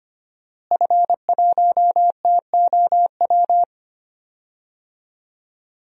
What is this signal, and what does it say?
Morse code, heard as a keyed tone, about 25 wpm, 700 Hz. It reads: F1TOW
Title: DXCC-100 Call Signs - 5 Characters - Repeated 2x - 25wpm